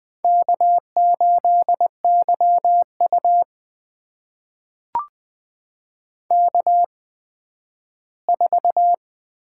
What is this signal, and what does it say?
Morse code, heard as a keyed tone, about 21 wpm, spaced as for 12 wpm, 700 Hz. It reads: K8YU K4
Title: USA Call Signs - 4 Characters - 8x Character Spacing - 20wpm